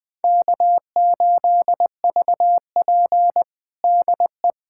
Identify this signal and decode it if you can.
Morse code, heard as a keyed tone, about 20 wpm, 700 Hz. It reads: K8VP DE
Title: USA Callsign DE Callsign - 4 Characters - 20wpm